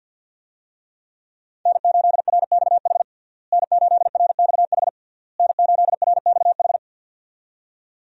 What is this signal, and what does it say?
Morse code, heard as a keyed tone, about 52 wpm, 700 Hz. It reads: N8RXH N8RXH N8RXH
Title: USA Call Signs - 5 Characters - Repeated 3x - 50wpm